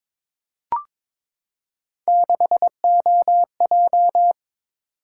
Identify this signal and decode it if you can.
Morse code, heard as a keyed tone, about 22 wpm, 700 Hz. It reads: 6OJ
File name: practice / 22wpm / Three Letter-Number Combinations - 22wpm